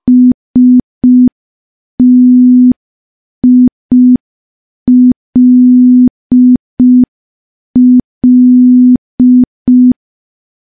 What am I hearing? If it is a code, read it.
Morse code, heard as a keyed tone, about 5 wpm, 250 Hz. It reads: STILL
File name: morse / morse-training-05WPM